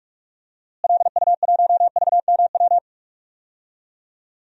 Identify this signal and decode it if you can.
Morse code, heard as a keyed tone, about 46 wpm, 700 Hz. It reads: LU1VMW
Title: DXCC-100 Call Signs - 6 Characters - Repeated 3x - 45wpm